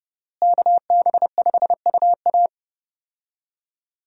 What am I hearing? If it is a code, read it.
Morse code, heard as a keyed tone, about 31 wpm, 700 Hz. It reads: KB5UA
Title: USA Call Signs - 5 Characters - Repeated 3x - 30wpm